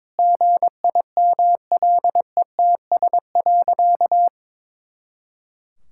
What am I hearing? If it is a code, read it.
Morse code, heard as a keyed tone, about 22 wpm, 700 Hz. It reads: GIMLETS.
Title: Pangram Challenge v2 - 22wpm